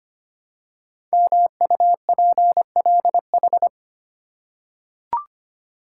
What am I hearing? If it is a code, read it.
Morse code, heard as a keyed tone, about 25 wpm, 700 Hz. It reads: MUPLH
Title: Code Groups - 5 Letters - 25wpm